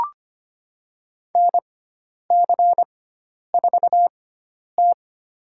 Morse code, heard as a keyed tone, about 25 wpm, spaced as for 10 wpm, 700 Hz. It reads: NC4T
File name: Lesson 16g - Letter c - Call Signs 25wpm